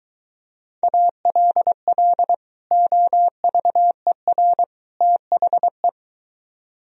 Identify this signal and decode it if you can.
Morse code, heard as a keyed tone, about 23 wpm, 700 Hz. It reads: ALL OVER THE